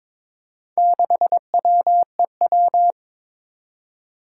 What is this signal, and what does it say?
Morse code, heard as a keyed tone, about 22 wpm, 700 Hz. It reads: 6WEW
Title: Four Challenging Letter-Number Combinations 22wpm